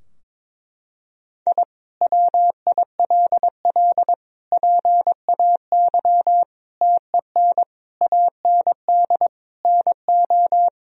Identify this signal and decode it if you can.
Morse code, heard as a keyed tone, about 22 wpm, 700 Hz. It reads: I WILL PAY TEN AND NO
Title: Sentences from Top 1000 Words - Review Part 2 - 22wpm